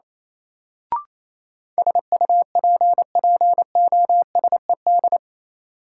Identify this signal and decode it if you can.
Morse code, heard as a keyed tone, about 28 wpm, 700 Hz. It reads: SUPPOSED
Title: Sets of 3 Words - New Word Drill - 28wpm